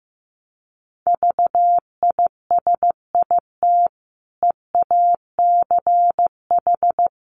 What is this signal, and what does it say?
Morse code, heard as a keyed tone, about 15 wpm, 700 Hz. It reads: VISIT EACH
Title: Sentences from Top 1100 Words 15wpm